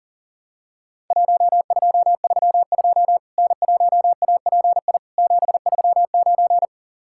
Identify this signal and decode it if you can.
Morse code, heard as a keyed tone, about 40 wpm, 700 Hz. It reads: 1232 N1API 739
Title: Worked All Europe DX Contest - QTC Exchange - 40wpm